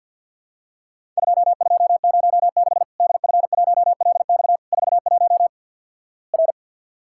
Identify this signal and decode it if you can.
Morse code, heard as a keyed tone, about 52 wpm, 700 Hz. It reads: 2206 DF1LX 41 R
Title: Worked All Europe DX Contest - QTC Exchange with Repeats - 50wpm